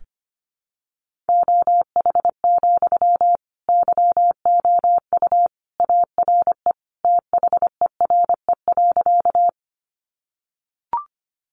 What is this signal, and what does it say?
Morse code, heard as a keyed tone, about 25 wpm, 700 Hz. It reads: OH, YOU ARE THERE.
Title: Sentences from Top 500 Words 25wpm